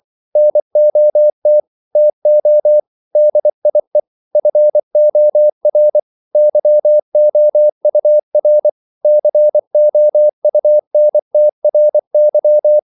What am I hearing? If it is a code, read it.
Morse code, heard as a keyed tone, about 24 wpm, 600 Hz. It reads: NOT TO DIE FOR YOUR COUNTRY
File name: CW / Practice / EnglishQuotations0324